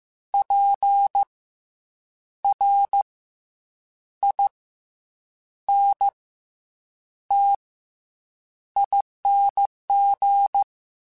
Morse code, heard as a keyed tone, about 15 wpm, spaced as for 7 wpm, 800 Hz. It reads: PRINTING
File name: Special INGs-mp3s / Special INGs-mp3s / Fast ing @13